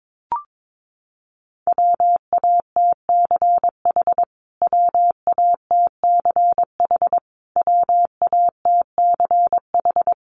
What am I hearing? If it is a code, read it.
Morse code, heard as a keyed tone, about 22 wpm, 700 Hz. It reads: WATCH WATCH WATCH
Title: Top 500 Words Repeated 22wpm